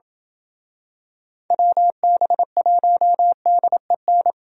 Morse code, heard as a keyed tone, about 27 wpm, 700 Hz. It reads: WB1DEN